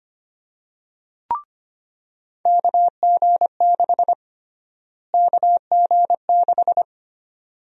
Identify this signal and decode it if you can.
Morse code, heard as a keyed tone, about 25 wpm, 700 Hz. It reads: KG6 KG6
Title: USA Call Sign Prefixes - Repeated 2x - 25wpm